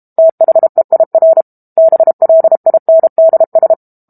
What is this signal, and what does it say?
Morse code, heard as a keyed tone, about 32 wpm, 650 Hz. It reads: THEIR BLINDS
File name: Book Two / 06-Chapter 6 - 32 WPM